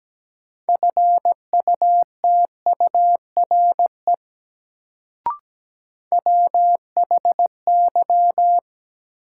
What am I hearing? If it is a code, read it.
Morse code, heard as a keyed tone, about 17 wpm, 700 Hz. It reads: FUTURE WHY
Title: Sets of 3 Words - New Word Drill - 17wpm